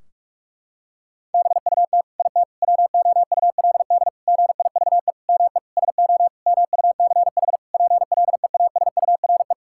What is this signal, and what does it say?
Morse code, heard as a keyed tone, about 41 wpm, 700 Hz. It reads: BUT IT WOULD GIVE ME SO MUCH PLEASURE
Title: Sentences from Top 1600 Words 45wpm